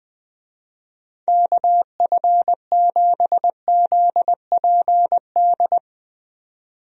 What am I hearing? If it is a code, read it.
Morse code, heard as a keyed tone, about 20 wpm, 700 Hz. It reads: KF7ZPD
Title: USA Call Signs - 6 Characters - 20wpm